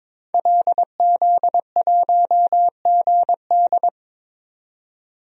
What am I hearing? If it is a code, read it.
Morse code, heard as a keyed tone, about 22 wpm, 700 Hz. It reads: LZ1GD
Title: DXCC-100 Call Signs - 5 Characters - 22wpm